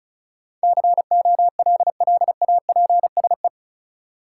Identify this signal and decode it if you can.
Morse code, heard as a keyed tone, about 34 wpm, 700 Hz. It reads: COLLAPSE
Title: Sets of 5 Words Encore - New Word Drill - 35wpm